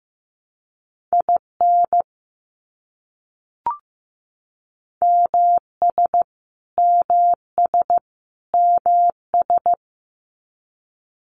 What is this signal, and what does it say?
Morse code, heard as a keyed tone, about 15 wpm, 700 Hz. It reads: IN MS MS MS
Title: US States Repeated 3x 15wpm